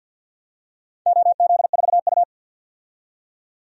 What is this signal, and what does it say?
Morse code, heard as a keyed tone, about 48 wpm, 700 Hz. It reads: OZ4U
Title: DXCC-100 Call Signs - 4 Characters - 50wpm